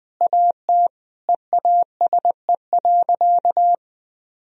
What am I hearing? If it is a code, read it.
Morse code, heard as a keyed tone, about 20 wpm, 700 Hz. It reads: AT EASE.
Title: Sentences from Top 1000 Words - Review Part 1 - 20wpm